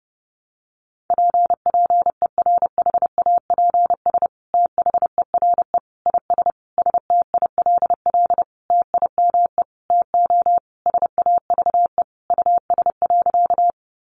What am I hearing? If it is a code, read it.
Morse code, heard as a keyed tone, about 30 wpm, 700 Hz. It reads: PPERHAPS THERE IS STILL TIME TO SAVE US.
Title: Sentences from Top 1100 Words 30wpm